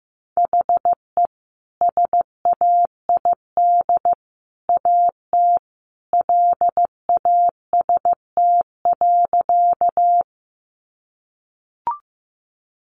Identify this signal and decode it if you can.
Morse code, heard as a keyed tone, about 14 wpm, 700 Hz. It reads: HE SAID AT LAST.
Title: Sentences from Top 300 Words 15wpm